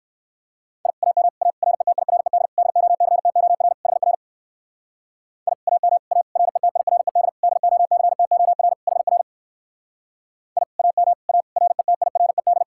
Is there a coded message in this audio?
Yes: Morse code from 0.9 to 12.6 s, heard as a keyed tone, about 59 wpm, 700 Hz. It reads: I AM A RETIRED DOCTOR HR I AM A RETIRED DOCTOR HR I AM A RETIRED